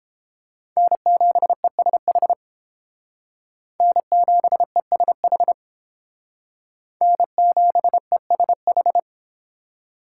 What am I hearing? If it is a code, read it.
Morse code, heard as a keyed tone, about 30 wpm, 700 Hz. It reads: N7ESH N7ESH N7ESH